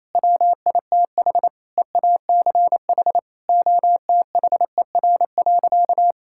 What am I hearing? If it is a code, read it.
Morse code, heard as a keyed tone, about 28 wpm, 700 Hz. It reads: WITH EACH OTHER.
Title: Sentences from Top 1000 Words - Review Part 6 - 28wpm